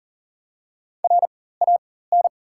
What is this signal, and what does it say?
Morse code, heard as a keyed tone, about 38 wpm, spaced as for 18 wpm, 700 Hz. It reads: RAN
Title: 4x Character Spacing - Top 500 Words - 40wpm